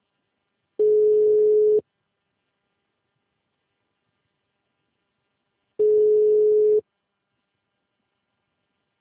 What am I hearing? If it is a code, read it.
Ringback tone (Europe, 425 Hz).